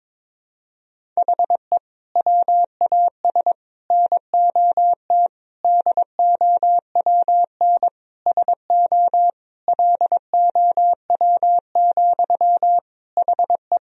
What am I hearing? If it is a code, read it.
Morse code, heard as a keyed tone, about 22 wpm, 700 Hz. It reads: HE WAS NOT DOWN SO LOW, HE
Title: Sentences from Top 1000 Words - Review Part 1 - 22wpm